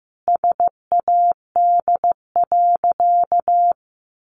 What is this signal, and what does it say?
Morse code, heard as a keyed tone, about 15 wpm, 700 Hz. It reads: SAD.
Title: Additional Easy Phrases 15wpm